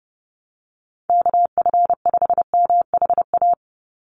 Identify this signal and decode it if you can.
Morse code, heard as a keyed tone, about 30 wpm, 700 Hz. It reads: KF5MHA
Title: USA Call Signs 30wpm